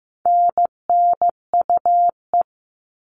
Morse code, heard as a keyed tone, about 15 wpm, 700 Hz. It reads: NNUE
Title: All Cut Numbers - 4-Digit 15wpm